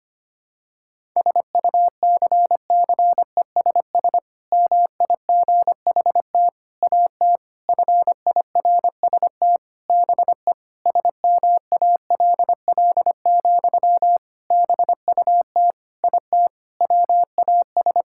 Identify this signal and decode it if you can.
Morse code, heard as a keyed tone, about 25 wpm, 700 Hz. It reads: SUCCESS MIGHT AT FIRST BE SMALL, BUT IT WAS
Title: Sentences from Top 1000 Words - Review Part 6 - 25wpm